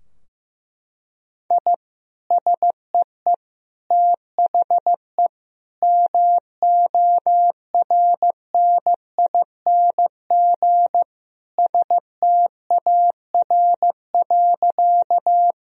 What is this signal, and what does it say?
Morse code, heard as a keyed tone, about 15 wpm, 700 Hz. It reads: I SEE THE MORNING STAR.
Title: Sentences from Top 1000 Words - Review Part 2 - 15wpm